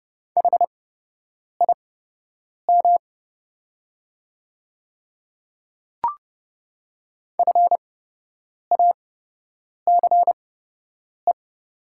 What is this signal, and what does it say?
Morse code, heard as a keyed tone, about 30 wpm, spaced as for 8 wpm, 700 Hz. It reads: HIM FACE
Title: 8x Character Spacing - Top 300 Words - 30wpm